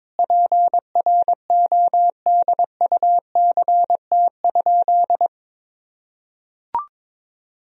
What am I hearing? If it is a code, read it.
Morse code, heard as a keyed tone, about 22 wpm, 700 Hz. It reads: PRODUCT?
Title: Sentences from Top 1000 Words - Review Part 3 - 22wpm